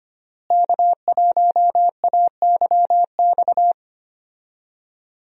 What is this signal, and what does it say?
Morse code, heard as a keyed tone, about 25 wpm, 700 Hz. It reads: K1AYX
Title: USA Call Signs - 5 Characters - Repeated 3x - 25wpm